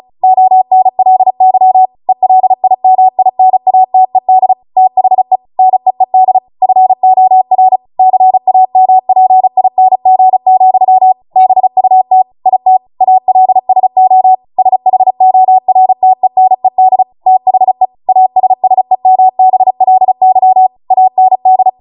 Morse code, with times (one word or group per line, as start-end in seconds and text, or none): 0.2-1.9 ONLY
2.1-4.5 ELIMINATED
4.8-5.4 THE
5.6-6.4 NEED
6.6-7.8 FOR
8.0-11.1 CAMPING,
11.4-12.2 BUT
12.5-12.8 IT
13.0-14.3 ALSO
14.6-17.0 SHORTENED
17.3-17.9 THE
18.1-20.7 ASSEMBLY
20.9-21.7 AND